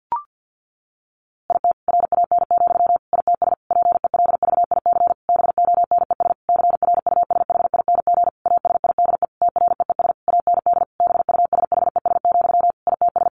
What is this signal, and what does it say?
Morse code, heard as a keyed tone, about 50 wpm, 700 Hz. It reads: IT RAN, ITS PELVIC BONES CRUSHING ASIDE TREES AND BUSHES, ITS
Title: A Sound Of Thunder 50wpm